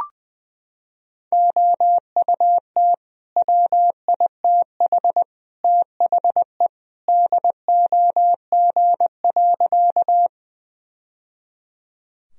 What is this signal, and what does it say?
Morse code, heard as a keyed tone, about 20 wpm, 700 Hz. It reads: OUT WITH THE DOG.
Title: Sentences from Top 1000 Words - Review Part 2 - 20wpm